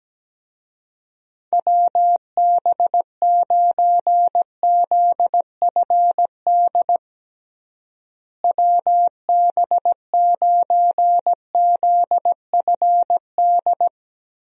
Morse code, heard as a keyed tone, about 17 wpm, 700 Hz. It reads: WB9ZFD WB9ZFD
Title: USA Call Signs - 6 Characters - Repeated 2x - 17wpm